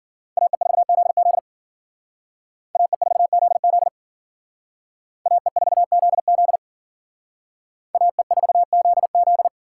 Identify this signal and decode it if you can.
Morse code, heard as a keyed tone, about 51 wpm, 700 Hz. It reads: AE4ZZ AE4ZZ AE4ZZ AE4ZZ